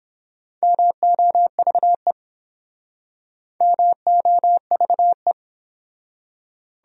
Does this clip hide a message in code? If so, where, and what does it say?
Yes: Morse code from 0.6 to 5.3 s, heard as a keyed tone, about 29 wpm, 700 Hz. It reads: MOVE MOVE